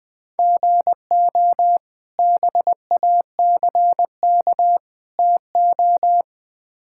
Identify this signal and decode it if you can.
Morse code, heard as a keyed tone, about 20 wpm, 700 Hz. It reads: GO BACK TO
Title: Build A Phrase - 4-Word Phrases - 20wpm